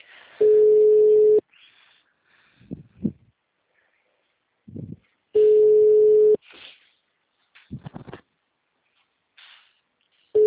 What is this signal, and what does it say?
Ringback tone (Europe, 425 Hz).
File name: out- -2006- - - .369